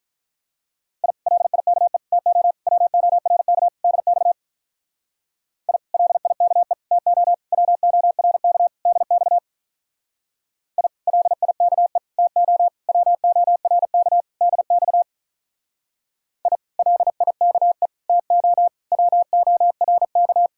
Morse code, heard as a keyed tone, about 44 wpm, 700 Hz. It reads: I LIKE TO WORK DX I LIKE TO WORK DX I LIKE TO WORK DX I LIKE TO WORK